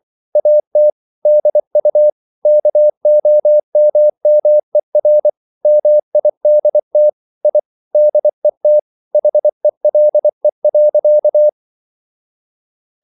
Morse code, heard as a keyed tone, about 24 wpm, 600 Hz. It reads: AT DU KOMMER MIDT I DET HELE.
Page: Ferie 24 wpm